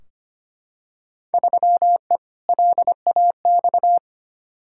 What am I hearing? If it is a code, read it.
Morse code, heard as a keyed tone, about 25 wpm, 700 Hz. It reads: 3E LAX
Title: Field Day Exchanges Encore - 25wpm